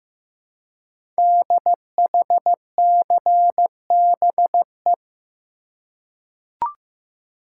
Morse code, heard as a keyed tone, about 15 wpm, 700 Hz. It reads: DHCBE
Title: Code Groups - 5 Letters - 15wpm